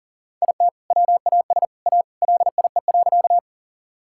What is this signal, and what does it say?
Morse code, heard as a keyed tone, about 40 wpm, 700 Hz. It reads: IT WAS A LIE.
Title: Sentences from Top 1600 Words 40wpm